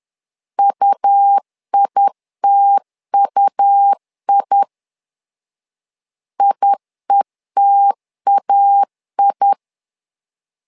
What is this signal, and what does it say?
Morse code, heard as a keyed tone, about 10 wpm, 800 Hz. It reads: UITUI IETAI
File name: LEZIONI / 2 UHDN / 2 U 10wpm 1622